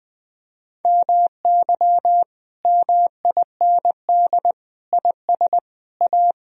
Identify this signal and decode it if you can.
Morse code, heard as a keyed tone, about 19 wpm, 700 Hz. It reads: MY MIND IS A
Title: Sentences from Top 1000 Words - Review Part 1 - 20wpm